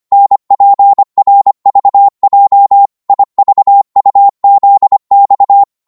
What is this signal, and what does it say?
Morse code, heard as a keyed tone, about 25 wpm, 800 Hz. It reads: NPRVJ IVUZX
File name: Morse / ltr-25wpm